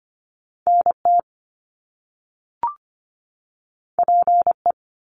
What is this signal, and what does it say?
Morse code, heard as a keyed tone, about 25 wpm, 700 Hz. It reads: NT PE
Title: Canadian Provinces and Territories 25wpm